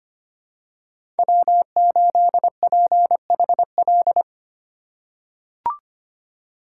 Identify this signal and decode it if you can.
Morse code, heard as a keyed tone, about 25 wpm, 700 Hz. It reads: W8PHL
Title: USA Call Signs - 5 Characters - 3x Character Spacing - 25wpm